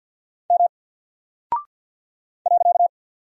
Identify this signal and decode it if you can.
Morse code, heard as a keyed tone, about 50 wpm, 700 Hz. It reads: M .
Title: Single Character (Letter:Number:Punctuation:Prosign) Alt 50wpm